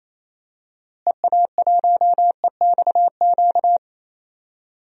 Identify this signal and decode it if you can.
Morse code, heard as a keyed tone, about 27 wpm, 700 Hz. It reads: EA1EXQ
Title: DXCC-100 Call Signs - 6 Characters - Repeated 2x - 28wpm